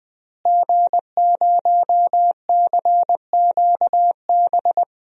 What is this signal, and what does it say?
Morse code, heard as a keyed tone, about 20 wpm, 700 Hz. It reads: G0CQB
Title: DXCC-100 Call Signs - 5 Characters - Repeated 2x - 20wpm